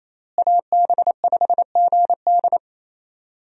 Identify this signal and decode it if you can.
Morse code, heard as a keyed tone, about 28 wpm, 700 Hz. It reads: AB5GD
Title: USA Call Signs - 5 Characters - Repeated 2x - 28wpm